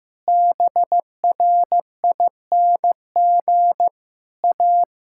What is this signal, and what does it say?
Morse code, heard as a keyed tone, about 16 wpm, 700 Hz. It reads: BRING A
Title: Sentences from Top 1000 Words - Review Part 5 - 15wpm